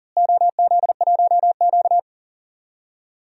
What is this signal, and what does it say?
Morse code, heard as a keyed tone, about 40 wpm, 700 Hz. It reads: OZ1Q
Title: DXCC-100 Call Signs - 4 Characters - Repeated 3x - 40wpm